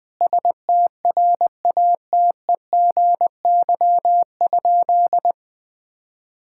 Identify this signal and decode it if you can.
Morse code, heard as a keyed tone, about 21 wpm, 700 Hz. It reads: STRATEGY?
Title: Sentences from Top 1000 Words - Review Part 5 - 20wpm